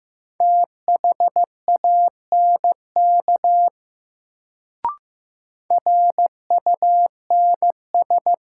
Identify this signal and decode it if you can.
Morse code, heard as a keyed tone, about 16 wpm, 700 Hz. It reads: THANK RUNS
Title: Sentences from Top 600 Words - New Word Drill - 15wpm